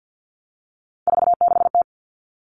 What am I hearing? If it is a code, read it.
Morse code, heard as a keyed tone, about 50 wpm, 700 Hz. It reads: 46T
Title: Common Cut Numbers - 3-Digit 50wpm